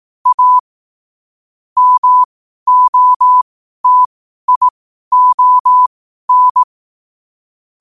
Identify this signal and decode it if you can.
Morse code, heard as a keyed tone, about 18 wpm, spaced as for 12 wpm, 1000 Hz. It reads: A MOTION
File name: qotdfiles / 12wpm